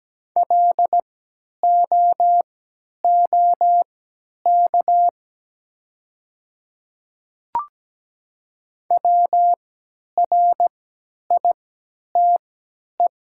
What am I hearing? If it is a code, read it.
Morse code, heard as a keyed tone, about 17 wpm, spaced as for 10 wpm, 700 Hz. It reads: LOOK WRITE
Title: 3x Character Spacing - Top 200 Words - 17wpm